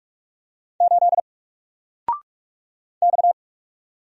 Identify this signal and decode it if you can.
Morse code, heard as a keyed tone, about 44 wpm, 700 Hz. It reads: 8 X
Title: Single Character (Letter:Number:Punctuation:Prosign) 45wpm